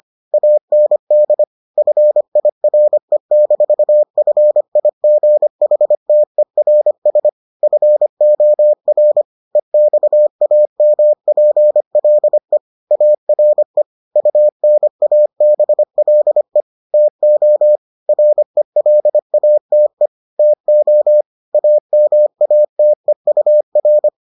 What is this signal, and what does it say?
Morse code, heard as a keyed tone, about 25 wpm, 600 Hz. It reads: AND FIRE-FIGHTERS FOR EXAMPLE ARE UNABLE TO RELATE TO AMATEUR